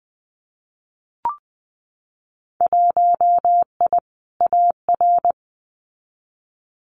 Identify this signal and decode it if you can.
Morse code, heard as a keyed tone, about 20 wpm, 700 Hz. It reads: 1I AR
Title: Winter Field Day Exchanges Encore 20wpm